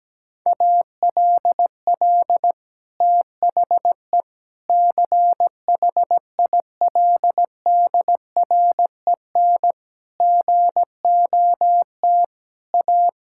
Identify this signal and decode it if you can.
Morse code, heard as a keyed tone, about 17 wpm, 700 Hz. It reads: ALL THE CHILDREN GOT A
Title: Sentences from Top 1000 Words - Review Part 6 - 17wpm